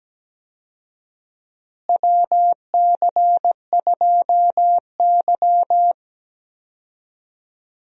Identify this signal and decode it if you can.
Morse code, heard as a keyed tone, about 17 wpm, 700 Hz. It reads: WC2Y
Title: USA Call Signs - 4 Characters - Repeated 2x - 17wpm